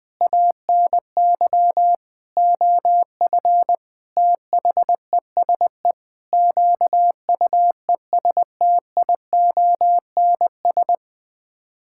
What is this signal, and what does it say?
Morse code, heard as a keyed tone, about 20 wpm, 700 Hz. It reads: ANY OF THESE QUESTIONS
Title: Sets of 9 Words - 20wpm